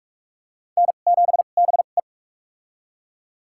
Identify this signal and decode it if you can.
Morse code, heard as a keyed tone, about 44 wpm, spaced as for 32 wpm, 700 Hz. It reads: N7BE
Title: USA Call Signs - 4 Characters - 2x Character Spacing - 45wpm